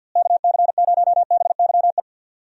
Morse code, heard as a keyed tone, about 46 wpm, 700 Hz. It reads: KC0BYE
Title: USA Call Signs - 6 Characters - Repeated 2x - 50wpm